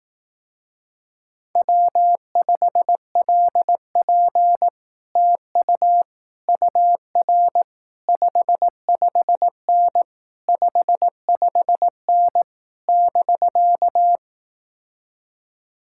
Morse code, heard as a keyed tone, about 18 wpm, 700 Hz. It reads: W5LP TU UR 55N 55N <BK>
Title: Speed-Racing Common QSO Phrases Encore - 15wpm